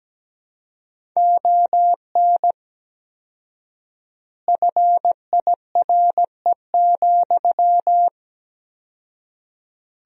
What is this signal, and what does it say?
Morse code, heard as a keyed tone, about 17 wpm, 700 Hz. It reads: ON FIRE,
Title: 4x Word-Spacing - Sentences from Top 400 Words - 17wpm